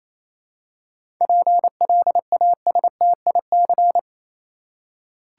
Morse code, heard as a keyed tone, about 28 wpm, 700 Hz. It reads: PLASTIC
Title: Seven Letter Words - 28wpm